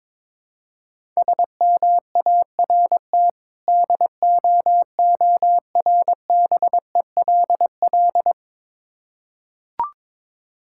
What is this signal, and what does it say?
Morse code, heard as a keyed tone, about 22 wpm, 690 Hz. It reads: SMART DOORBELL
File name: Difficult Words with Hints - 22wpm